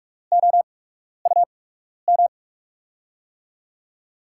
Morse code, heard as a keyed tone, about 44 wpm, spaced as for 12 wpm, 700 Hz. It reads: OUM
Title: USA Call Signs - 5 Characters - 8x Character Spacing - 45wpm